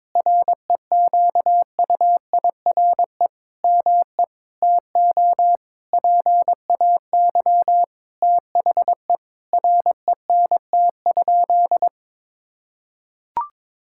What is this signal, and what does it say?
Morse code, heard as a keyed tone, about 22 wpm, 700 Hz. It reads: REQUIRE ME TO PAY THE RENT?
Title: Sentences from Top 1000 Words - Review Part 6 - 22wpm